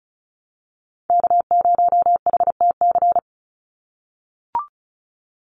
Morse code, heard as a keyed tone, about 34 wpm, 700 Hz. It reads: K0HTC
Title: USA Call Signs 35wpm